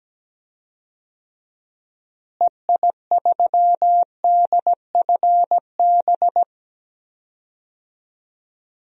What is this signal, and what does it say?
Morse code, heard as a keyed tone, about 17 wpm, 700 Hz. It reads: EI3DFB
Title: DXCC-100 Call Signs - 6 Characters - Repeated 3x - 17wpm